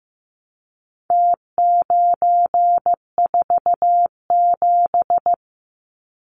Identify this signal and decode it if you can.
Morse code, heard as a keyed tone, about 15 wpm, 700 Hz. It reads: T947
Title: Numbers - Contest Style - 4-Digit 15wpm